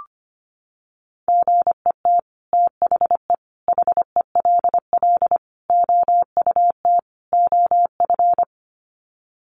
Morse code, heard as a keyed tone, about 25 wpm, 700 Hz. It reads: GET THE HELL OUT OF
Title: Sets of 5 Words Encore2 25wpm